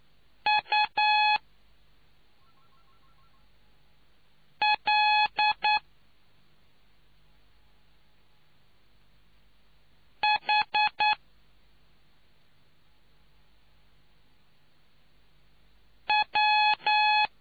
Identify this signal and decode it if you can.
Morse code, heard as a keyed tone, about 9 wpm, 840 Hz. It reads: U L H W